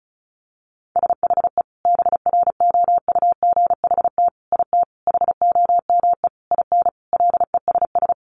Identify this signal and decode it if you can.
Morse code, heard as a keyed tone, about 34 wpm, 700 Hz. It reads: SHE BROUGHT IT HOME IN LESS
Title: Sentences from Top 500 Words 35wpm